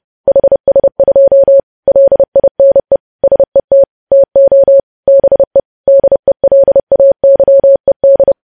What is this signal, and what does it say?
Morse code, heard as a keyed tone, about 30 wpm, 550 Hz. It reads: HS2 LINE SET TO BE DELAYED
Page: news headlines in morse code at 30 wpm